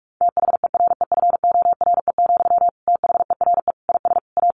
Morse code, heard as a keyed tone, about 44 wpm, 700 Hz. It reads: THEREFORE, THERE IS A
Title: Sentences from Top 1500 Words 45wpm